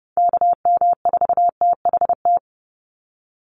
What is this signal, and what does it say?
Morse code, heard as a keyed tone, about 30 wpm, 700 Hz. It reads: KM4THT
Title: USA Call Signs 30wpm